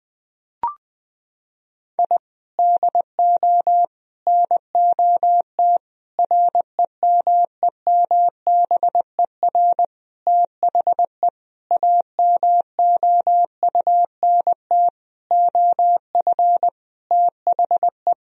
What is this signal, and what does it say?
Morse code, heard as a keyed tone, about 21 wpm, 700 Hz. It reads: I DO NOT REMEMBER THE AMOUNT OF THE
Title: Sentences from Top 1000 Words - Review Part 4 - 20wpm